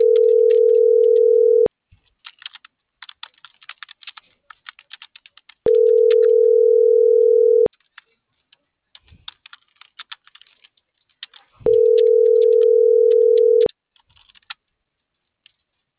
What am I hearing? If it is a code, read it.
Ringback tone (North America), 440 + 480 Hz.